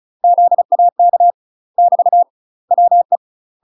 Morse code, heard as a keyed tone, about 36 wpm, 700 Hz. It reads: ZAK <BT> WE
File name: UK MASTER QSOs TRAINING / 35WPM / UK MASTER QSOs 0014